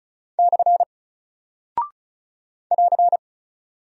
Morse code, heard as a keyed tone, about 36 wpm, 700 Hz. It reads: / <AR>
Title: Single Character (Letter:Number:Punctuation:Prosign) 35wpm